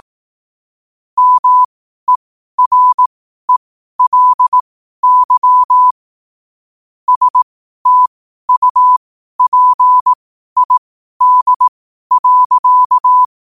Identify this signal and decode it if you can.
Morse code, heard as a keyed tone, about 18 wpm, spaced as for 12 wpm, 1000 Hz. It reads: MERELY STUPID.
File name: qotdfiles / 12wpm